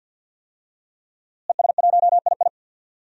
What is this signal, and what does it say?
Morse code, heard as a keyed tone, about 49 wpm, 700 Hz. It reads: ES1II
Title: DXCC-100 Call Signs - 5 Characters - Repeated 3x - 50wpm